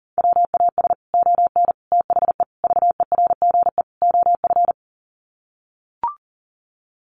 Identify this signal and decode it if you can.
Morse code, heard as a keyed tone, about 40 wpm, 700 Hz. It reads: WAS ON THE VERGE OF
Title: Sets of 5 Words Encore 40wpm